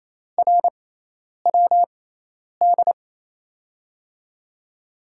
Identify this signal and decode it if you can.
Morse code, heard as a keyed tone, about 28 wpm, spaced as for 10 wpm, 700 Hz. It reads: RWD
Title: USA Call Signs - 6 Characters - 6x Character Spacing - 28wpm